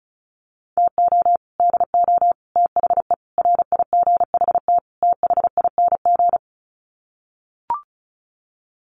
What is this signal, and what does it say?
Morse code, heard as a keyed tone, about 34 wpm, 700 Hz. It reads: TO DO THE RIGHT THING
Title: Sets of 5 Words 35wpm